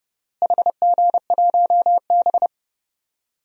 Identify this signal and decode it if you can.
Morse code, heard as a keyed tone, about 30 wpm, 700 Hz. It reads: HG1B
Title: DXCC-100 Call Signs - 4 Characters - Repeated 3x - 30wpm